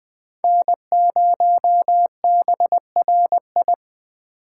Morse code, heard as a keyed tone, about 21 wpm, 700 Hz. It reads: N0BRI